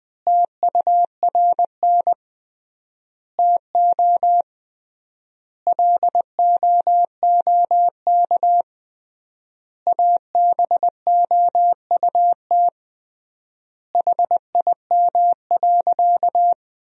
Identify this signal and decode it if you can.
Morse code, heard as a keyed tone, about 21 wpm, 700 Hz. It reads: TURN TO LOOK ABOUT HIM.
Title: 3x Word-Spacing - Sentences from Top 200 Words - 20wpm